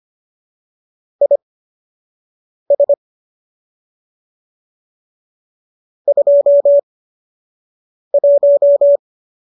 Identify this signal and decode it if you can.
Morse code, heard as a keyed tone, about 25 wpm, spaced as for 6 wpm, 570 Hz. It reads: IS 21